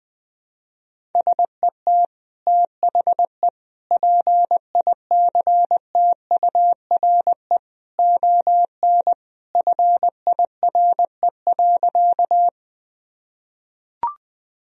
Morse code, heard as a keyed tone, about 20 wpm, 700 Hz. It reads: SET THE PICTURE ON FIRE.
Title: Sentences from Top 1000 Words - Review Part 1 - 20wpm